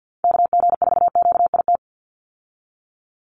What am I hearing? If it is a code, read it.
Morse code, heard as a keyed tone, about 50 wpm, 700 Hz. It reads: KG4QIT
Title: USA Call Signs 50wpm